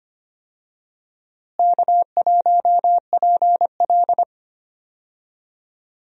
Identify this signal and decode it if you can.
Morse code, heard as a keyed tone, about 25 wpm, 700 Hz. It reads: K1PL
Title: USA Call Signs - 4 Characters - Repeated 2x - 25wpm